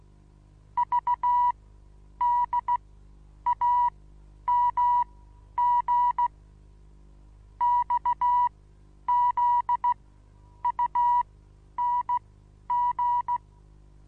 Morse code, heard as a keyed tone, about 16 wpm, spaced as for 9 wpm, 970 Hz. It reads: VDAMG XZUNG
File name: Onlinelehrgang / morsen / DJ4UF Morsen25